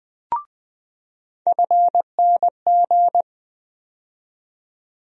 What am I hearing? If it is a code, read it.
Morse code, heard as a keyed tone, about 20 wpm, 700 Hz. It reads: FNG
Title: USA Call Sign Postfixes - 3 Letters 20wpm